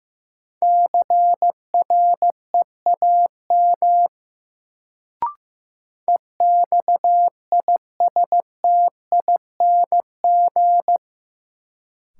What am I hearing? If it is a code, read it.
Morse code, heard as a keyed tone, about 15 wpm, 700 Hz. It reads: CREAM EXISTING
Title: Sentences from Top 1500 Words - New Word Drill Alt - 15wpm